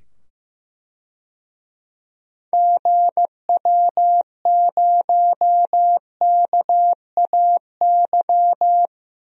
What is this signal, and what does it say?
Morse code, heard as a keyed tone, about 15 wpm, 700 Hz. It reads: GW0KAY
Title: DXCC-100 Call Signs - 6 Characters - 15wpm